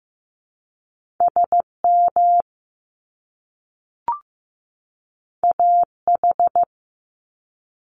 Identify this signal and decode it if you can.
Morse code, heard as a keyed tone, about 15 wpm, 700 Hz. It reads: SM AH
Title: Top Two Letter Combinations in Words 15wpm